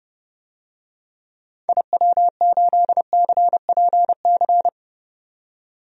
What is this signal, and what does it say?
Morse code, heard as a keyed tone, about 32 wpm, 700 Hz. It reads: IW8CPC
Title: DXCC-100 Call Signs - 6 Characters - 30wpm